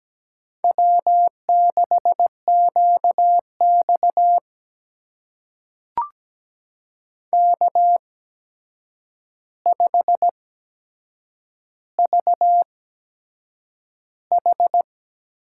Morse code, heard as a keyed tone, about 17 wpm, spaced as for 7 wpm, 700 Hz. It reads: W6QX K5VH
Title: USA Call Signs - 8x Character Spacing - 17wpm